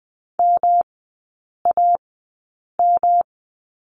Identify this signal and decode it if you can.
Morse code, heard as a keyed tone, about 20 wpm, spaced as for 8 wpm, 700 Hz. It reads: MAM
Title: 114 Lesson 19e - Letter w - Three Characters 20wpm